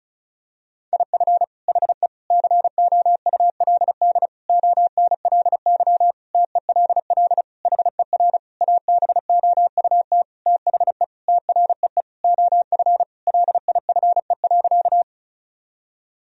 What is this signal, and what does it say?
Morse code, heard as a keyed tone, about 34 wpm, 700 Hz. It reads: IF HE COULD ONLY TELL HER ABOUT THE TREE OF LIFE.
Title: Sentences from Top 1000 Words - Review Part 1 - 35wpm